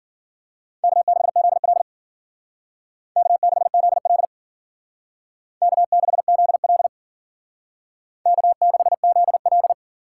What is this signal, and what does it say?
Morse code, heard as a keyed tone, about 48 wpm, 700 Hz. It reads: K6ZL K6ZL K6ZL K6ZL
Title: Speed Racing - USA 1x2 Call Signs 40wpm